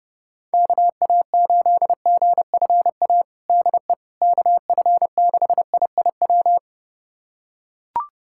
Morse code, heard as a keyed tone, about 31 wpm, 700 Hz. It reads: KA8GFA DE KF6IIW
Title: USA Callsign DE Callsign 30wpm